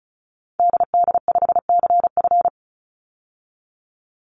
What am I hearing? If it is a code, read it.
Morse code, heard as a keyed tone, about 34 wpm, 700 Hz. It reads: DD5CF
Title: Call Signs Repeated 3x - 35wpm